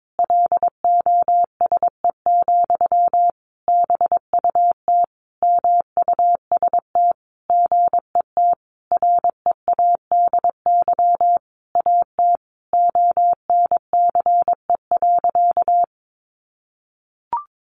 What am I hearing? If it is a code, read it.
Morse code, heard as a keyed tone, about 22 wpm, 700 Hz. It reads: LOSE, BUT MUST GET READY AT ONCE.
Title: Sentences from Top 1100 Words 22wpm